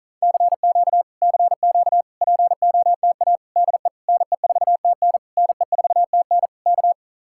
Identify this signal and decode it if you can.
Morse code, heard as a keyed tone, about 48 wpm, 700 Hz. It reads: CQ CQ POTA DE NE4TN NE4TN K